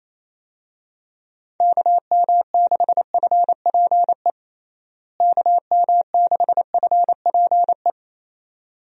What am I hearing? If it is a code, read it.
Morse code, heard as a keyed tone, about 28 wpm, 700 Hz. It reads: KM6FPE KM6FPE
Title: USA Call Signs - 6 Characters - Repeated 2x - 28wpm